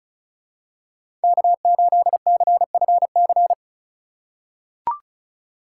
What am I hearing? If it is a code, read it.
Morse code, heard as a keyed tone, about 34 wpm, 700 Hz. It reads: K8CFC
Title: USA Call Signs - 5 Characters - 4x Character Spacing - 35wpm